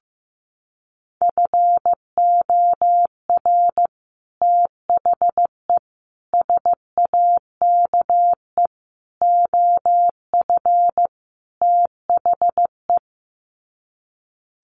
Morse code, heard as a keyed tone, about 15 wpm, 700 Hz. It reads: FOR THE SAKE OF THE
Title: Sets of 5 Words Encore 15wpm